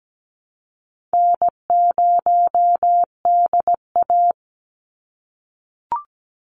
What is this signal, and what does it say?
Morse code, heard as a keyed tone, about 17 wpm, 700 Hz. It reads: N0DA
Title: NA SOTA Chaser Callsigns 17wpm